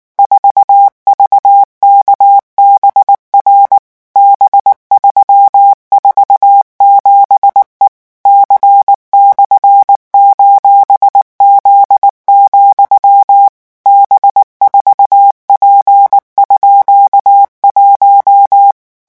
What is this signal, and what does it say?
Morse code, heard as a keyed tone, about 19 wpm, 800 Hz. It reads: 4VKBR B347E C/:Z, B4P_1
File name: morse mp3 / revision17-vitesse 18 mots